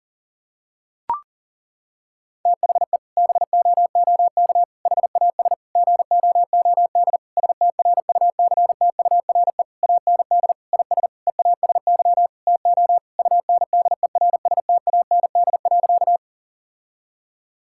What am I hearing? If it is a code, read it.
Morse code, heard as a keyed tone, about 40 wpm, 700 Hz. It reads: THE BOOK HAS GOOD STRUCTURE AND IS EASY TO UNDERSTAND.
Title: Sentences from Top 1000 Words - Review Part 6 - 40wpm